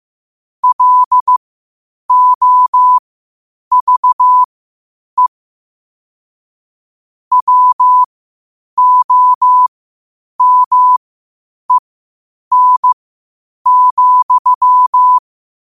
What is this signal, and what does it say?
Morse code, heard as a keyed tone, about 15 wpm, spaced as for 8 wpm, 1000 Hz. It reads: LOVE WOMEN,
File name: qotdfiles / 8wpm